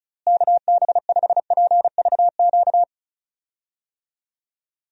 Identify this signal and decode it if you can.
Morse code, heard as a keyed tone, about 36 wpm, 700 Hz. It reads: KB5PVQ
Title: USA Call Signs - 6 Characters - Repeated 3x - 35wpm